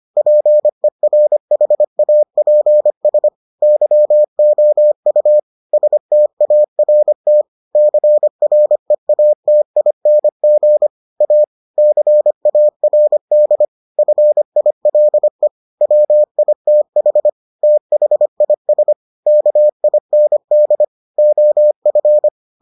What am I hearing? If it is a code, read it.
Morse code, heard as a keyed tone, about 25 wpm, 600 Hz. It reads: PERHAPS YOU START CREATING A CARD FILE WITH THIS KIND OF